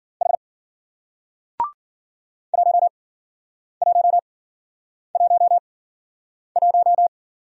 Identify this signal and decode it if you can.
Morse code, heard as a keyed tone, about 48 wpm, spaced as for 9 wpm, 700 Hz. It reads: H 1111
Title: Speed-Racing Single Letter-Number 40wpm